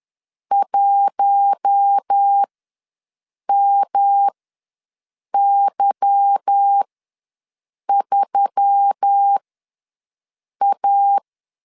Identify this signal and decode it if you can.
Morse code, heard as a keyed tone, about 10 wpm, spaced as for 6 wpm, 800 Hz. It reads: 1MY3A
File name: LEZIONI / 9 2 ws 10wpm 4130